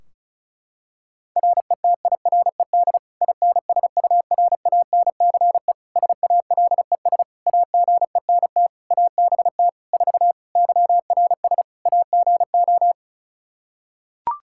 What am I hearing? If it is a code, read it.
Morse code, heard as a keyed tone, about 36 wpm, 700 Hz. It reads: RETIRED INSURANCE SALES AGENT ABT 4 YRS AGO
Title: K1USN SST QSO Practice 35wpm